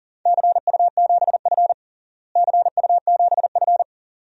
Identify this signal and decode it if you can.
Morse code, heard as a keyed tone, about 40 wpm, 700 Hz. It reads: CU7F CU7F
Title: DXCC-100 Call Signs - 4 Characters - Repeated 2x - 40wpm